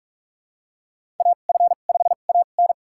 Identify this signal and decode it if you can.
Morse code, heard as a keyed tone, about 44 wpm, spaced as for 32 wpm, 700 Hz. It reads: AF5AN
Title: USA Call Signs - 5 Characters - 2x Character Spacing - 45wpm